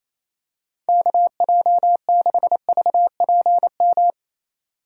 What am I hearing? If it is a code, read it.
Morse code, heard as a keyed tone, about 28 wpm, 700 Hz. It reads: KJ6VPM
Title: USA Call Signs - 2x Character Spacing - 28wpm